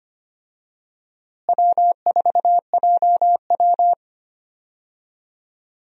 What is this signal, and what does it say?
Morse code, heard as a keyed tone, about 25 wpm, 700 Hz. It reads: W4JW
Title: USA Call Signs - 4 Characters - Repeated 3x - 25wpm